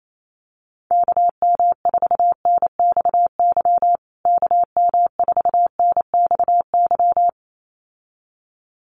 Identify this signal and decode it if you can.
Morse code, heard as a keyed tone, about 28 wpm, 700 Hz. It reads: KM4NXY KM4NXY
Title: USA Call Signs Repeated 2x 28wpm